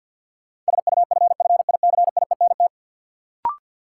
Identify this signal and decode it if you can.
Morse code, heard as a keyed tone, about 53 wpm, 700 Hz. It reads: SUFFICIENT
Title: Sentences from Top 1400 Words - New Word Drill - 50wpm